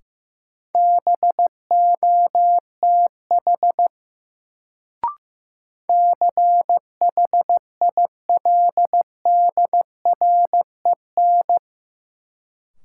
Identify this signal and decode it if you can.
Morse code, heard as a keyed tone, about 15 wpm, 700 Hz. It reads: BOTH CHILDREN
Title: Sentences from Top 300 Words - New Word Drill - 15wpm